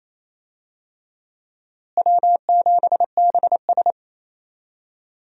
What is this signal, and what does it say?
Morse code, heard as a keyed tone, about 28 wpm, 700 Hz. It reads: W7BS
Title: USA Call Signs - 4 Characters - Repeated 2x - 28wpm